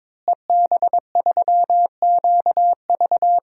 Morse code, heard as a keyed tone, about 22 wpm, 700 Hz. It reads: EB3QV
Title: DXCC-100 Call Signs - 5 Characters - Repeated 3x - 22wpm